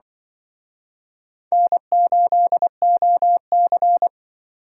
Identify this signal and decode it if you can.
Morse code, heard as a keyed tone, about 25 wpm, 700 Hz. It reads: N8OC
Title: Speed Racing - USA 1x2 Call Signs 20wpm